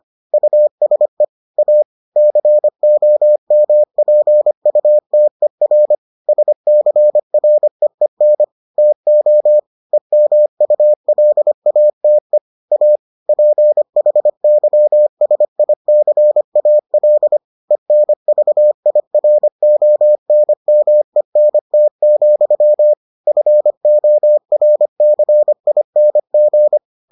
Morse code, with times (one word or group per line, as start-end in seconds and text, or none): 0.3-1.2 USE
1.6-1.8 A
2.2-6.0 COMPUTER
6.3-8.4 SCREEN
8.8-9.6 TO
9.9-12.4 EMULATE
12.7-13.0 A
13.3-17.4 PHYSICAL
17.7-22.9 ENVIRONMENT,
23.3-26.8 FORCING